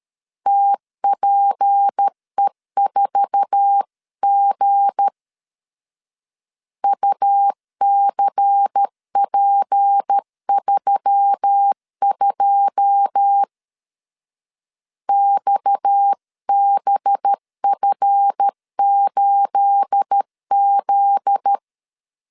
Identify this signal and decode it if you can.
Morse code, heard as a keyed tone, about 13 wpm, 800 Hz. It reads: TPE4G UCP32 XBF8Z
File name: CW / DET / 1 9 1500C S B 12wpm 3819